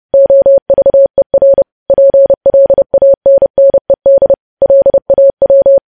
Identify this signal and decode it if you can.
Morse code, heard as a keyed tone, about 30 wpm, 550 Hz. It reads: OVER PLANNED LAW
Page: news headlines in morse code at 30 wpm